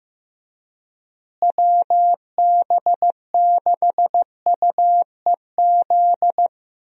Morse code, heard as a keyed tone, about 15 wpm, 700 Hz. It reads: WB6UEZ
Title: USA Call Signs - 6 Characters - Repeated 2x - 15wpm